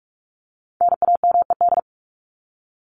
Morse code, heard as a keyed tone, about 44 wpm, 700 Hz. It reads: NAMED
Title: ED Suffix - Top 100 Words with Base Words 45wpm